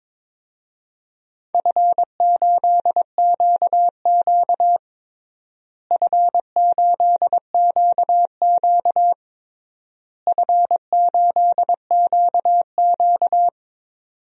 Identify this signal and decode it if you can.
Morse code, heard as a keyed tone, about 22 wpm, 700 Hz. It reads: F8QQ F8QQ F8QQ
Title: DXCC-100 Call Signs - 4 Characters - Repeated 3x - 22wpm